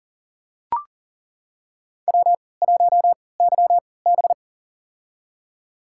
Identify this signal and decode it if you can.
Morse code, heard as a keyed tone, about 40 wpm, spaced as for 23 wpm, 690 Hz. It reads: W1YB
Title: USA Call Signs - 3x Character Spacing - 40wpm